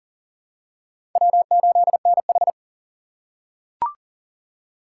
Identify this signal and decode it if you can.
Morse code, heard as a keyed tone, about 39 wpm, 700 Hz. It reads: W8NH
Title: USA Call Signs - 3x Character Spacing - 40wpm